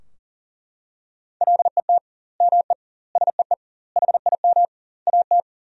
Morse code, heard as a keyed tone, about 40 wpm, 700 Hz. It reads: LET ME SEE HIM AT
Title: 2x Word-Spacing - Sentences from Top 300 Words - 40wpm